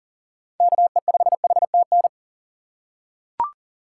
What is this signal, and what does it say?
Morse code, heard as a keyed tone, about 40 wpm, 700 Hz. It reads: KE5HTN
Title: USA Call Signs - 6 Characters - 3x Character Spacing - 40wpm